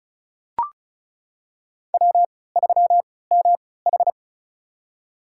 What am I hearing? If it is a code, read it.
Morse code, heard as a keyed tone, about 36 wpm, spaced as for 20 wpm, 700 Hz. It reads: W3MH
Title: USA Call Signs - 4 Characters - 3x Character Spacing - 35wpm